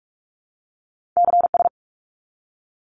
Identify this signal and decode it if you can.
Morse code, heard as a keyed tone, about 44 wpm, 700 Hz. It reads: CS
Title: Abbreviations 45wpm